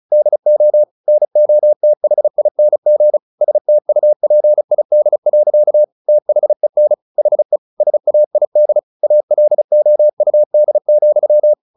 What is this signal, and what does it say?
Morse code, heard as a keyed tone, about 34 wpm, 600 Hz. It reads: DO NOTHING STUPID. THEN HE SAID ALOUD,
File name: the old man and the sea / the old man and the sea 1 35wpm